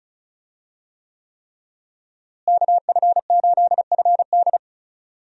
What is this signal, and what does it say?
Morse code, heard as a keyed tone, about 36 wpm, 700 Hz. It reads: KF8FD
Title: USA Call Signs - 5 Characters - Repeated 3x - 35wpm